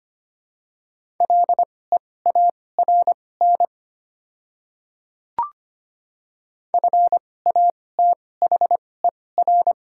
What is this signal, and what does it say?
Morse code, heard as a keyed tone, about 25 wpm, spaced as for 18 wpm, 700 Hz. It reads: LEARN FATHER
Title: 2x Character Spacing - Top 300 Words - 25wpm